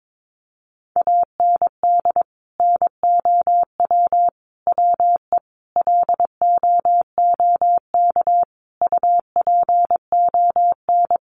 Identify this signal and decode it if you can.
Morse code, heard as a keyed tone, about 22 wpm, 700 Hz. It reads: AND NOW WE LOOK UPON
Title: Sentences from Top 1200 Words 22wpm